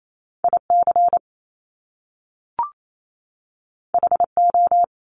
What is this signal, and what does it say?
Morse code, heard as a keyed tone, about 28 wpm, 700 Hz. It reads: IC HO
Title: Top Two Letter Combinations in Words 28wpm